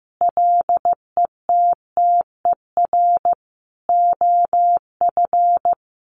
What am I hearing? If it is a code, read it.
Morse code, heard as a keyed tone, about 15 wpm, 700 Hz. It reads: LETTER OF
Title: Sentences from Top 1400 Words 15wpm